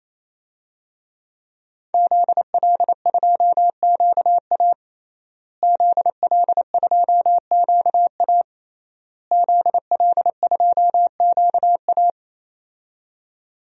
Morse code, heard as a keyed tone, about 28 wpm, 700 Hz. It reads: ZL2QA ZL2QA ZL2QA
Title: DXCC-100 Call Signs - 5 Characters - Repeated 3x - 28wpm